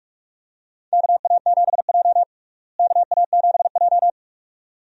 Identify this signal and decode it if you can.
Morse code, heard as a keyed tone, about 46 wpm, 700 Hz. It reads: KA7J KA7J
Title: USA Call Signs - 4 Characters - Repeated 2x - 45wpm